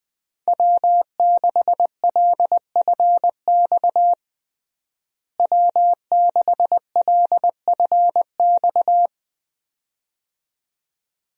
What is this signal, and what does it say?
Morse code, heard as a keyed tone, about 20 wpm, 700 Hz. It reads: W6LFX W6LFX
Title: USA Call Signs - 5 Characters - Repeated 2x - 20wpm